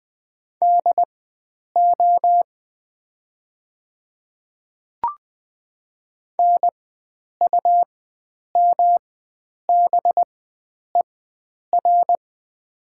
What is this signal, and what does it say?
Morse code, heard as a keyed tone, about 20 wpm, spaced as for 9 wpm, 700 Hz. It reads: DO NUMBER
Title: 4x Character Spacing - Top 100 Words - 20wpm